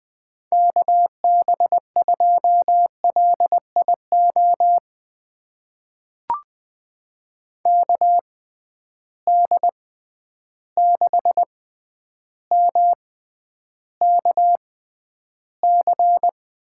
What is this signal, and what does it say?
Morse code, heard as a keyed tone, about 20 wpm, spaced as for 10 wpm, 700 Hz. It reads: KB2LIO KD6MKC
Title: USA Call Signs - 6 Characters - 6x Character Spacing - 20wpm